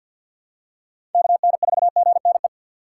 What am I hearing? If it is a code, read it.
Morse code, heard as a keyed tone, about 51 wpm, 690 Hz. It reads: KN4GNE
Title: USA Call Signs - 2x Character Spacing - 50wpm